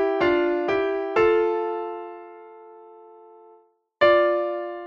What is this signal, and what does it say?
Dial tone (North America), 350 + 440 Hz.